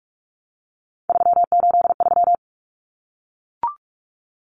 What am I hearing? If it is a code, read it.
Morse code, heard as a keyed tone, about 44 wpm, 700 Hz. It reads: 383
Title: Numbers - 3-Digit 45wpm